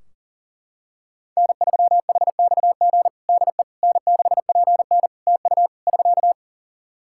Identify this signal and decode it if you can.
Morse code, heard as a keyed tone, about 40 wpm, 700 Hz. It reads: N3HXG DE N6PN TU <SK>
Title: K1USN SST QSO Practice 40wpm